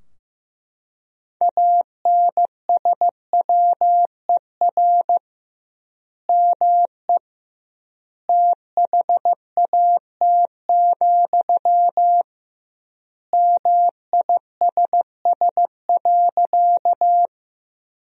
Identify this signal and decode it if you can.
Morse code, heard as a keyed tone, about 15 wpm, 700 Hz. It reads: ANSWER ME THAT, MISS.
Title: 2x Word-Spacing - Sentences from Top 500 Words - 15wpm